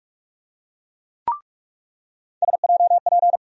Morse code, heard as a keyed tone, about 42 wpm, 700 Hz. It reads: SJP
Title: USA Call Sign Postfixes - 3 Letters 45wpm